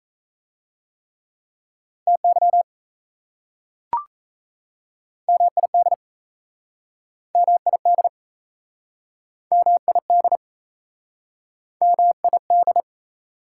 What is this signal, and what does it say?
Morse code, heard as a keyed tone, about 38 wpm, 700 Hz. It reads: TY MID MID MID MID